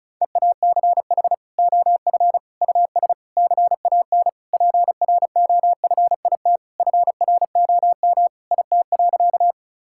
Morse code, heard as a keyed tone, about 34 wpm, 700 Hz. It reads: EACH OF US CAN PROFIT FROM IT.
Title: Sentences from Top 1000 Words - Review Part 5 - 35wpm